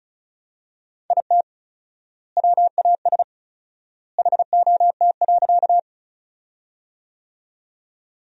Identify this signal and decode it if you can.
Morse code, heard as a keyed tone, about 36 wpm, 700 Hz. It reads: IT WAS HOT.
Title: 4x Word-Spacing - Sentences from Top 500 Words - 35wpm